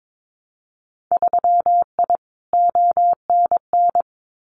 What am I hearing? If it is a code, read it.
Morse code, heard as a keyed tone, about 22 wpm, 700 Hz. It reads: 3I ONN
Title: Winter Field Day Exchanges Encore 22wpm